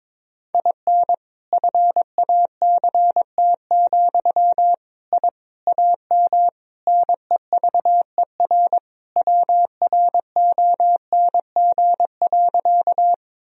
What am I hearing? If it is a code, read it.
Morse code, heard as a keyed tone, about 21 wpm, 700 Hz. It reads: IN FACT, I AM NEVER WRONG.
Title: Sentences from Top 1000 Words - Review Part 3 - 22wpm